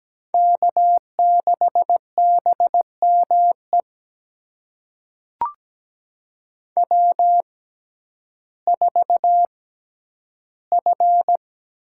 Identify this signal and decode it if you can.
Morse code, heard as a keyed tone, about 17 wpm, spaced as for 10 wpm, 700 Hz. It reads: K6BME W4F
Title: USA Call Signs - 5 Characters - 6x Character Spacing - 17wpm